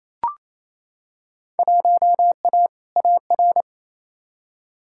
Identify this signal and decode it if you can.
Morse code, heard as a keyed tone, about 27 wpm, 700 Hz. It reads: 1A AR
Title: Field Day Exchanges Encore - 28wpm